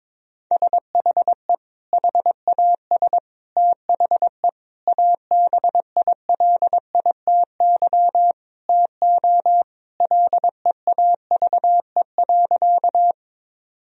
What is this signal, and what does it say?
Morse code, heard as a keyed tone, about 22 wpm, 700 Hz. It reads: SHE HAS THE ABILITY TO LEAVE.
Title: Sentences from Top 1000 Words - Review Part 3 - 22wpm